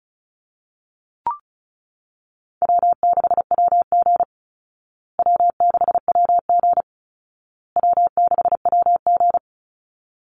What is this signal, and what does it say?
Morse code, heard as a keyed tone, about 34 wpm, 700 Hz. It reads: W6WG W6WG W6WG
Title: Easy Call Signs Repeated 3x - 35wpm